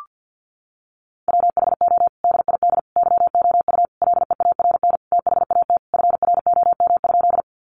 Morse code, heard as a keyed tone, about 50 wpm, 700 Hz. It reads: WHO DID YOU LEARN THAT FROM?
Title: Sentences from Top 300 Words 50wpm